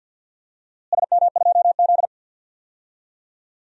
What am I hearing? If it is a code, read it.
Morse code, heard as a keyed tone, about 52 wpm, 700 Hz. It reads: SM2Z
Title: DXCC-100 Call Signs - 4 Characters - Repeated 3x - 50wpm